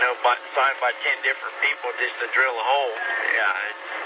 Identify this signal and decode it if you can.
Emergency Alert System (SAME) tones, heard as harsh buzzing bursts, with speech. It is heard from 0.9 to 3.7 s.